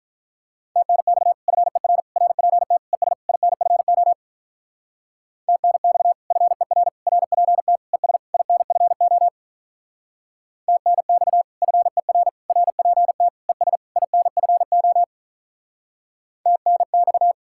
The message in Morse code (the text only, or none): TNX FER RPT ES INFO TNX FER RPT ES INFO TNX FER RPT ES INFO TNX